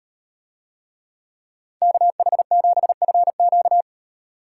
Morse code, heard as a keyed tone, about 38 wpm, 700 Hz. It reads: KH7FQ